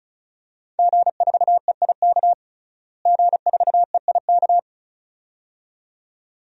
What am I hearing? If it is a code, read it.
Morse code, heard as a keyed tone, about 34 wpm, 700 Hz. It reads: G4EIK G4EIK
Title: DXCC-100 Call Signs - 5 Characters - Repeated 2x - 35wpm